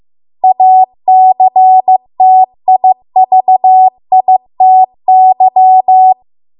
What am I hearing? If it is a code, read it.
Morse code, heard as a keyed tone, about 15 wpm, 750 Hz. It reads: ACTIVITY